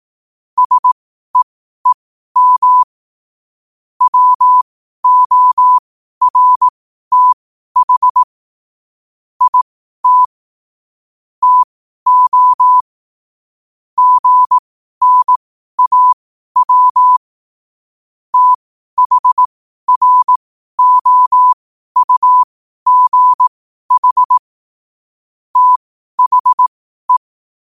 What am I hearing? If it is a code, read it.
Morse code, heard as a keyed tone, about 18 wpm, spaced as for 12 wpm, 1000 Hz. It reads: SEEM WORTH IT TO GNAW THROUGH THE
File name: qotdfiles / 12wpm